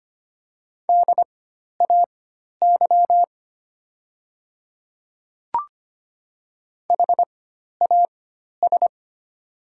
Morse code, heard as a keyed tone, about 25 wpm, spaced as for 12 wpm, 700 Hz. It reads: DAY HAS
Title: 4x Character Spacing - Top 100 Words - 25wpm